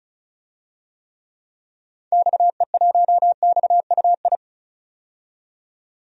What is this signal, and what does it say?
Morse code, heard as a keyed tone, about 36 wpm, 700 Hz. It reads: XE1XUI
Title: DXCC-100 Call Signs - 6 Characters - 35wpm